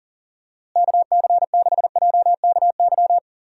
Morse code, heard as a keyed tone, about 40 wpm, 700 Hz. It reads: KC6JKY
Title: USA Call Signs - 6 Characters - Repeated 2x - 40wpm